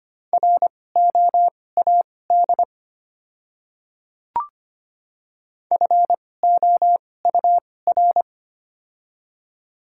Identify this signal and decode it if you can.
Morse code, heard as a keyed tone, about 25 wpm, spaced as for 18 wpm, 690 Hz. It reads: ROAD FOUR
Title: 2x Character Spacing - Top 500 Words - 25wpm